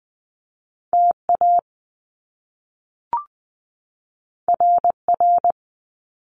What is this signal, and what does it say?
Morse code, heard as a keyed tone, about 20 wpm, 700 Hz. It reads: T A R R
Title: Top Two Letter Combinations in Words 20wpm